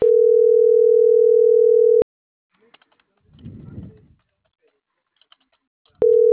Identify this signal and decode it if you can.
Ringback tone (North America), 440 + 480 Hz.